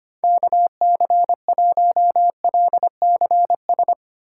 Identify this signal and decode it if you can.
Morse code, heard as a keyed tone, about 25 wpm, 700 Hz. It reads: KC1LCS
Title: USA Call Signs - 6 Characters - 3x Character Spacing - 25wpm